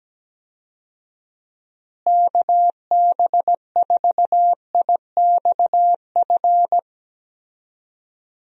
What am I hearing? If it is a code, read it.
Morse code, heard as a keyed tone, about 17 wpm, 700 Hz. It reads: KB4IXF
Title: USA Call Signs - 6 Characters - Repeated 3x - 17wpm